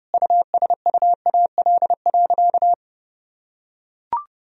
Morse code, heard as a keyed tone, about 30 wpm, 700 Hz. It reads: USUAL.
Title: Sentences from Top 1000 Words - Review Part 2 - 30wpm